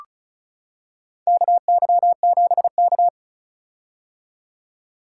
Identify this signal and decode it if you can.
Morse code, heard as a keyed tone, about 36 wpm, 700 Hz. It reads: KY7K
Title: K1USN SST QSO Practice - 35wpm